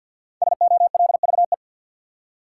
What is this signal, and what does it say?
Morse code, heard as a keyed tone, about 52 wpm, 700 Hz. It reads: SOLVE
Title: Five Letter Words - 50wpm